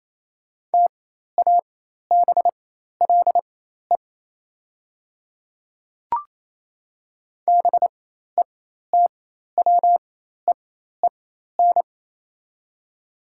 Morse code, heard as a keyed tone, about 27 wpm, spaced as for 13 wpm, 700 Hz. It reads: TABLE BETWEEN
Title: 4x Character Spacing - Top 500 Words - 28wpm